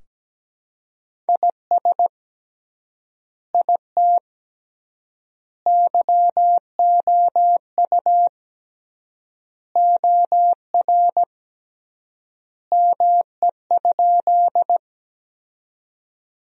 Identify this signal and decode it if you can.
Morse code, heard as a keyed tone, about 17 wpm, 700 Hz. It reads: IS IT YOU OR ME?
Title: 3x Word-Spacing - Sentences from Top 200 Words - 17wpm